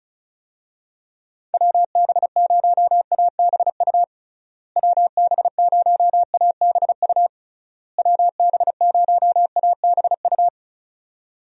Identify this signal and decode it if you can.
Morse code, heard as a keyed tone, about 34 wpm, 700 Hz. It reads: WB0ABU WB0ABU WB0ABU
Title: USA Call Signs - 6 Characters - Repeated 3x - 35wpm